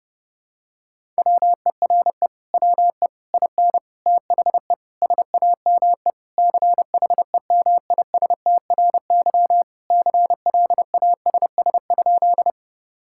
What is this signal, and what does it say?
Morse code, heard as a keyed tone, about 30 wpm, 700 Hz. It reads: WERE WE IN THE SAME CHEMISTRY CLASS?
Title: Sentences from Top 1000 Words - Review Part 6 - 30wpm